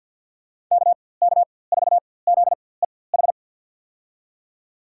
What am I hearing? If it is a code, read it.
Morse code, heard as a keyed tone, about 51 wpm, spaced as for 23 wpm, 700 Hz. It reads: KK4ZEH
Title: USA Call Signs - 6 Characters - 4x Character Spacing - 50wpm